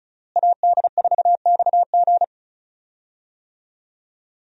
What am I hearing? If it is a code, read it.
Morse code, heard as a keyed tone, about 34 wpm, 700 Hz. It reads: AD4XG
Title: USA Call Signs - 5 Characters - Repeated 2x - 35wpm